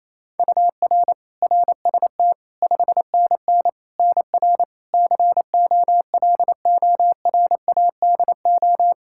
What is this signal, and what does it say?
Morse code, heard as a keyed tone, about 28 wpm, 700 Hz. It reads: UR RST 5NN NR COLORADO
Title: Practice QSO - Exchange 1 - Beginning - 28wpm